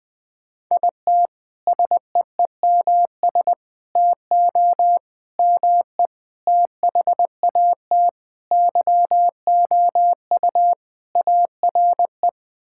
Morse code, heard as a keyed tone, about 20 wpm, 700 Hz. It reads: IT SEEMS TO ME THAT YOU ARE
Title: Sentences with 5-Word Phrases - 20wpm